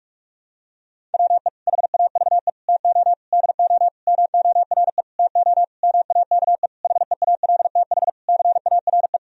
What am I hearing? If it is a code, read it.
Morse code, heard as a keyed tone, about 46 wpm, 700 Hz. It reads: WE HAVE TO DO MORE TO MAKE HEALTH CARE
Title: Sentences from Top 1000 Words - Review Part 3 - 45wpm